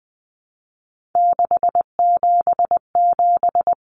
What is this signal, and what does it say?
Morse code, heard as a keyed tone, about 20 wpm, 700 Hz. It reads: 677
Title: Common Cut Numbers - 3-Digit 20wpm